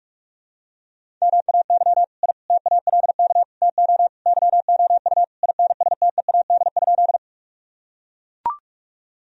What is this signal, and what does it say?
Morse code, heard as a keyed tone, about 41 wpm, 700 Hz. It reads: MAY I TALK TO YOU INSTEAD?
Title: Sentences from Top 1000 Words - Review Part 4 - 45wpm